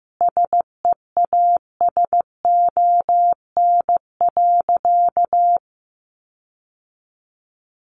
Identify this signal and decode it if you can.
Morse code, heard as a keyed tone, about 15 wpm, 700 Hz. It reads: SEASON.
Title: Sentences from Top 1400 Words 15wpm